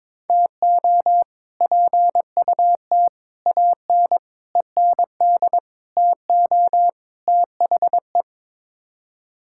Morse code, heard as a keyed tone, about 21 wpm, 700 Hz. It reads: TO PUT AN END TO THE
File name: practice / 22wpm / Sets of 6 Words - 22wpm